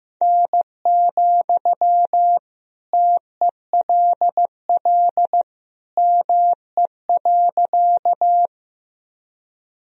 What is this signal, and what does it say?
Morse code, heard as a keyed tone, about 15 wpm, 700 Hz. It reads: N, TELL ME.
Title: Sentences from Top 200 Words and Abbreviations 15wpm